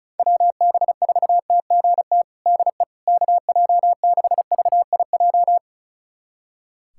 Morse code, heard as a keyed tone, about 36 wpm, 700 Hz. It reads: WB4TGT DE KJ6VIJ
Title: USA Callsign DE Callsign - 6 Characters - 35wpm